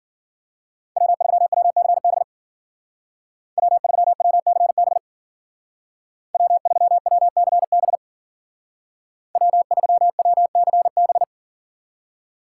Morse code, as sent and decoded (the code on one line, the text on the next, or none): .-- ...-- .-- -.-. -... / .-- ...-- .-- -.-. -... / .-- ...-- .-- -.-. -... / .-- ...-- .-- -.-. -...
W3WCB W3WCB W3WCB W3WCB